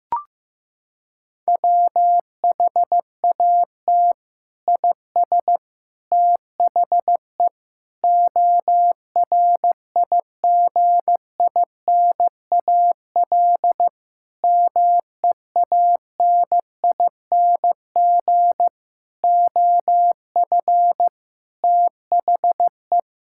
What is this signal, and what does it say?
Morse code, heard as a keyed tone, about 15 wpm, 700 Hz. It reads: WHAT IS THE ORIGINAL MEANING OF THE
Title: Sentences from Top 1000 Words - Review Part 6 - 15wpm